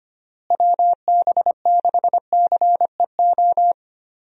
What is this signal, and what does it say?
Morse code, heard as a keyed tone, about 25 wpm, 700 Hz. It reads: WB6CEO
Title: USA Call Signs - 6 Characters - Repeated 3x - 25wpm